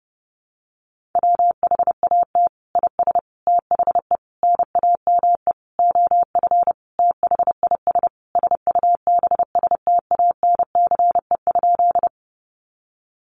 Morse code, heard as a keyed tone, about 30 wpm, 700 Hz. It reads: WHAT IS THE NAME OF THIS SUBSTANCE?
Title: Sentences from Top 1500 Words 30wpm